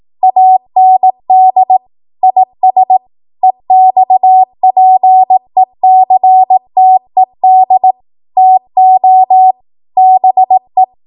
Morse code, heard as a keyed tone, about 18 wpm, 750 Hz. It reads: AND IS EXPECTED TO BE
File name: Morse / 18WPM